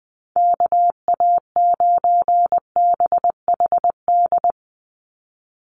Morse code, heard as a keyed tone, about 20 wpm, 700 Hz. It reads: KA9BHD
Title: Call Signs Repeated 2x - 20wpm